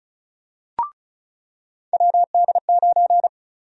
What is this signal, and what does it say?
Morse code, heard as a keyed tone, about 36 wpm, 700 Hz. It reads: WD9
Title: USA Call Sign Prefixes - 35wpm